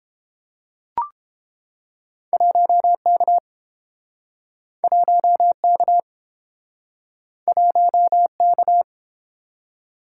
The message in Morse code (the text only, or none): 1K 1K 1K